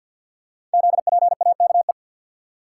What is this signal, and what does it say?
Morse code, heard as a keyed tone, about 47 wpm, 690 Hz. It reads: ZPAKE
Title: Code Groups - 5 Letter:Numbers - 50wpm